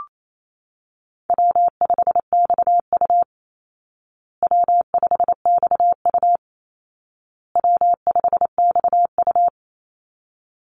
Morse code, heard as a keyed tone, about 28 wpm, 700 Hz. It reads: W5XU W5XU W5XU
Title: Easy Call Signs Repeated 3x - 28wpm